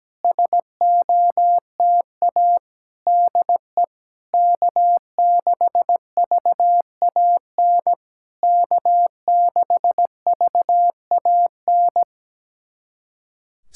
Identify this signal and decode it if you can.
Morse code, heard as a keyed tone, about 17 wpm, 700 Hz. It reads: SOTA DE K6VAN K6VAN
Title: SOTA QSO Practice 17wpm